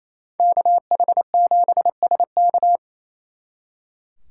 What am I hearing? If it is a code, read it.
Morse code, heard as a keyed tone, about 36 wpm, 700 Hz. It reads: KH7SK